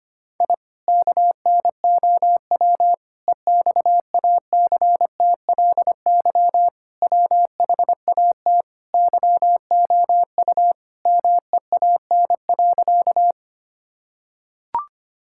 Morse code, heard as a keyed tone, about 25 wpm, 700 Hz. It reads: I KNOW EXACTLY WHAT YOU MEAN.
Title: Sentences from Top 1000 Words - Review Part 6 - 25wpm